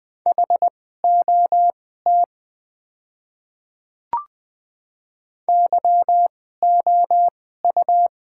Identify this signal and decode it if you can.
Morse code, heard as a keyed tone, about 20 wpm, spaced as for 14 wpm, 700 Hz. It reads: HOT YOU
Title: 2x Character Spacing - Top 200 Words - 20wpm